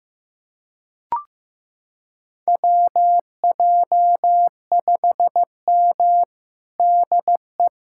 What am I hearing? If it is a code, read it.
Morse code, heard as a keyed tone, about 15 wpm, 700 Hz. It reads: WJ5M DE
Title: USA Callsign DE Callsign - 4 Characters - 15wpm